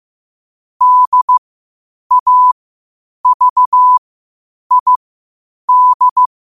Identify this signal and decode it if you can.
Morse code, heard as a keyed tone, about 15 wpm, spaced as for 8 wpm, 1000 Hz. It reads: DAVID
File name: qotdfiles / 8wpm